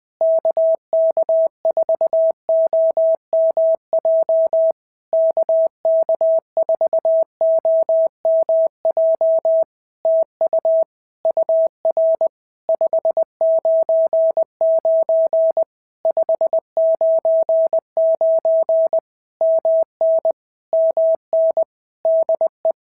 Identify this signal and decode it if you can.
Morse code, heard as a keyed tone, about 20 wpm, 650 Hz. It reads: KK4OMJ KK4OMJ TU UR 599 599 MN MN DE
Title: POTA QSO Practice - 20wpm